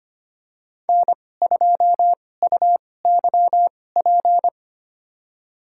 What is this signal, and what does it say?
Morse code, heard as a keyed tone, about 25 wpm, spaced as for 18 wpm, 700 Hz. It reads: N2UYP
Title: USA Call Signs - 5 Characters - 2x Character Spacing - 25wpm